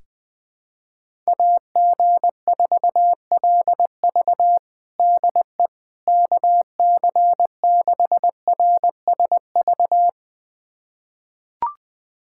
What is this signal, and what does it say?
Morse code, heard as a keyed tone, about 20 wpm, 700 Hz. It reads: AG4LV DE KC6RSV
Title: USA Callsign DE Callsign 20wpm